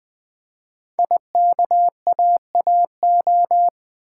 Morse code, heard as a keyed tone, about 20 wpm, 700 Hz. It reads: IKAAO
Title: Code Groups - 5 Letters - 20wpm